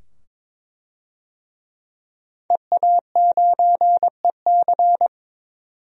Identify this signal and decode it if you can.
Morse code, heard as a keyed tone, about 22 wpm, 700 Hz. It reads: EA9EC
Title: DXCC-100 Call Signs - 5 Characters - 22wpm